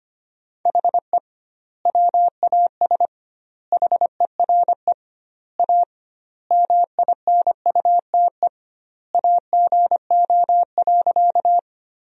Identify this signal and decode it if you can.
Morse code, heard as a keyed tone, about 25 wpm, 700 Hz. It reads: HE WAS HERE A MINUTE AGO.
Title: 2x Word-Spacing - Sentences from Top 500 Words - 25wpm